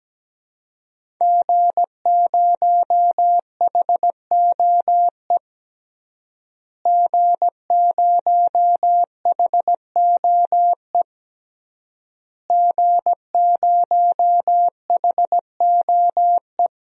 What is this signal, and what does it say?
Morse code, heard as a keyed tone, about 17 wpm, 700 Hz. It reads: G0HOE G0HOE G0HOE
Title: DXCC-100 Call Signs - 5 Characters - Repeated 3x - 17wpm